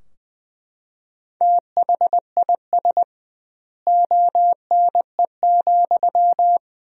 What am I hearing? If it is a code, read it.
Morse code, heard as a keyed tone, about 20 wpm, 700 Hz. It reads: THIS ONE,
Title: 2x Word-Spacing - Sentences from Top 200 Words - 20wpm